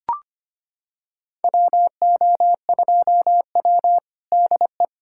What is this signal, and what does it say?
Morse code, heard as a keyed tone, about 25 wpm, 700 Hz. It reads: WO2W DE
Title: USA Callsign DE Callsign - 4 Characters - 25wpm